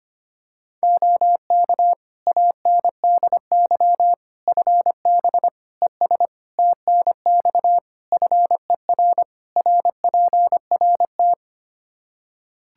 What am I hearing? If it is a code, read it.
Morse code, heard as a keyed tone, about 25 wpm, 700 Hz. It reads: OK ANDY FB ES TNX FER RPRT
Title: Practice QSO - Exchange 2 - Info - 25wpm